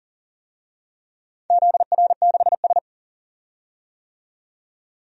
Morse code, heard as a keyed tone, about 40 wpm, 700 Hz. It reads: ZR6S
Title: DXCC-100 Call Signs - 4 Characters - 40wpm